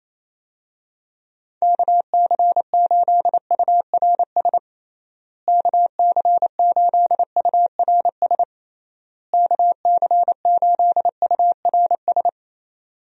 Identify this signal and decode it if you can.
Morse code, heard as a keyed tone, about 28 wpm, 700 Hz. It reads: KC8URS KC8URS KC8URS
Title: USA Call Signs - 6 Characters - Repeated 3x - 28wpm